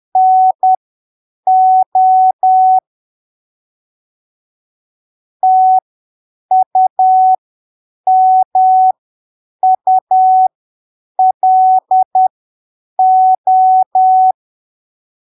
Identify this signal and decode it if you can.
Morse code, heard as a keyed tone, about 10 wpm, spaced as for 7 wpm, 750 Hz. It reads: NO TUMULO
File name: arquivos / texto / texto 27 10ppm